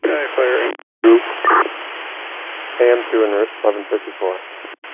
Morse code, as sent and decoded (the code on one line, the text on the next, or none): none